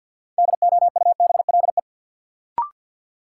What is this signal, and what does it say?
Morse code, heard as a keyed tone, about 53 wpm, 700 Hz. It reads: DOUBLE
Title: Sentences from Top 1400 Words - New Word Drill - 50wpm